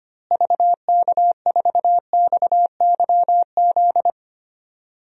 Morse code, heard as a keyed tone, about 25 wpm, 700 Hz. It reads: VK4XYZ
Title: DXCC-100 Call Signs - 6 Characters - Repeated 3x - 25wpm